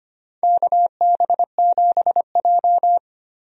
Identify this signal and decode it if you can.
Morse code, heard as a keyed tone, about 25 wpm, 700 Hz. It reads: KB7J
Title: USA Call Signs - 4 Characters - 25wpm